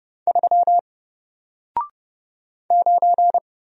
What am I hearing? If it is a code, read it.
Morse code, heard as a keyed tone, about 30 wpm, 700 Hz. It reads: 3 9
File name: Single-Letter-Number-30wpm